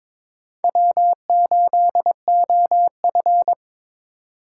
Morse code, heard as a keyed tone, about 22 wpm, 700 Hz. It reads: W8OF
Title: USA Call Signs - 4 Characters - 2x Character Spacing - 22wpm